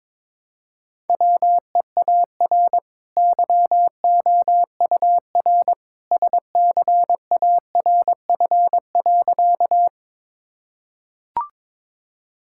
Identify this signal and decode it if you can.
Morse code, heard as a keyed tone, about 22 wpm, 700 Hz. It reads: WEAR YOUR SCARF.
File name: practice / 22wpm / Easy Sentences with Hints - 22wpm